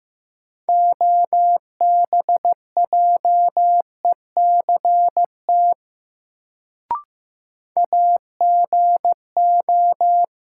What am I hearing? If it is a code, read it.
Morse code, heard as a keyed tone, about 15 wpm, 700 Hz. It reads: OBJECT AGO
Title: Sentences from Top 500 Words - New Word Drill - 15wpm